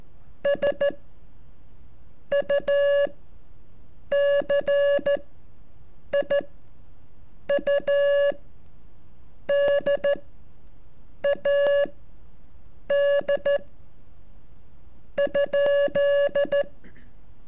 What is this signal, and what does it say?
Morse code, heard as a keyed tone, about 13 wpm, spaced as for 6 wpm, 570 Hz. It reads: SU CIUDAD ?
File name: curso / ejercicio 34